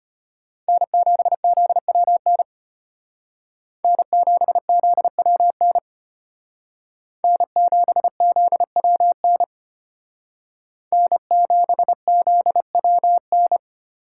Morse code, heard as a keyed tone, about 31 wpm, 700 Hz. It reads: N7ZWN N7ZWN N7ZWN N7ZWN
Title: Speed Racing - USA 1x3 Call Signs 25wpm